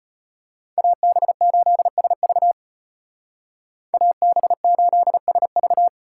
Morse code, heard as a keyed tone, about 30 wpm, 700 Hz. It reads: AB8SV AB8SV
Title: Speed Racing - USA 2x2 Call Signs 25wpm